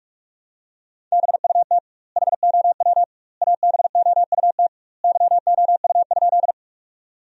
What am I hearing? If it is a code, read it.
Morse code, heard as a keyed tone, about 46 wpm, 700 Hz. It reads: BUT HOW ABOUT YOU?
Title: 2x Word-Spacing - Sentences from Top 100 Words - 45wpm